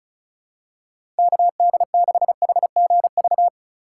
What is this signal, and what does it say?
Morse code, heard as a keyed tone, about 34 wpm, 700 Hz. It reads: KD6HGV
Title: USA Call Signs - 6 Characters - 2x Character Spacing - 35wpm